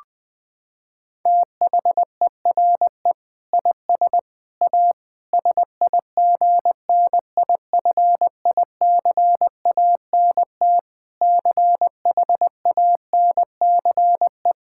Morse code, heard as a keyed tone, about 20 wpm, 700 Hz. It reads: THERE IS A SIGNIFICANT CHANCE
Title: Sentences from Top 1000 Words - Review Part 6 - 20wpm